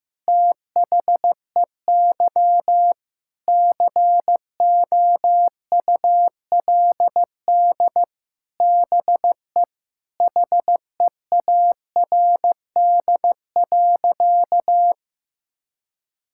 Morse code, heard as a keyed tone, about 15 wpm, 700 Hz. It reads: THEY COULD BE HEARD.
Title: Sentences from Top 1000 Words - Review Part 5 - 15wpm